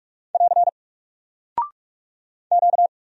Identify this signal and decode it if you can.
Morse code, heard as a keyed tone, about 46 wpm, 700 Hz. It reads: <AR> Q
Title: Single Character (Letter:Number:Punctuation:Prosign) Alt 45wpm